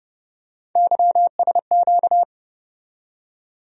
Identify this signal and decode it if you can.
Morse code, heard as a keyed tone, about 30 wpm, 700 Hz. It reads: YSQ
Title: USA Call Sign Postfixes - 3 Letters 30wpm